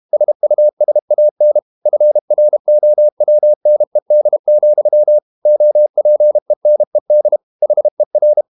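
Morse code, heard as a keyed tone, about 32 wpm, 600 Hz. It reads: SUSAN FROWNED, OPENED HER
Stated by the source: Morse code practice at 32 wpm from CW med Gnister, reading Ken Follett's novel Never in English.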